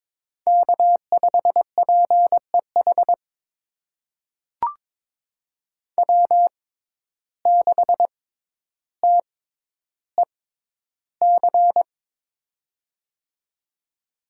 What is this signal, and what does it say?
Morse code, heard as a keyed tone, about 22 wpm, spaced as for 11 wpm, 700 Hz. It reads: K5PEH W6TEC
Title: USA Call Signs - 5 Characters - 6x Character Spacing - 22wpm